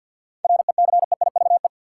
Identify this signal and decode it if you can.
Morse code, heard as a keyed tone, about 53 wpm, 700 Hz. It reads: RECEIVE
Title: Sentences from Top 1000 Words - New Word Drill - 50wpm